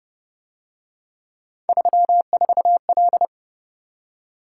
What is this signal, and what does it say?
Morse code, heard as a keyed tone, about 29 wpm, 700 Hz. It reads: 34L